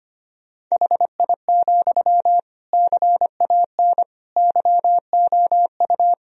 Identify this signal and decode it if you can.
Morse code, heard as a keyed tone, about 25 wpm, 700 Hz. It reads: HI, CAN YOU
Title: Sentences from Top 1000 Words - Review Part 3 - 25wpm